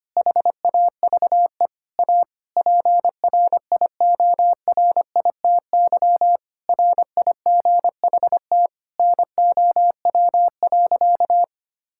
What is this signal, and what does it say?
Morse code, heard as a keyed tone, about 25 wpm, 700 Hz. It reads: HAVE A PRIORITY RIGHT NOW.
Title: Sentences from Top 1600 Words 25wpm